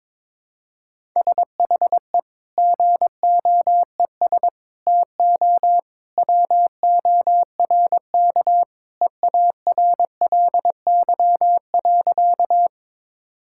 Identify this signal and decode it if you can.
Morse code, heard as a keyed tone, about 22 wpm, 700 Hz. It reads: SHE GOES TO WORK EARLY.
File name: practice / 22wpm / Sentences with 2-Word Phrases - 22wpm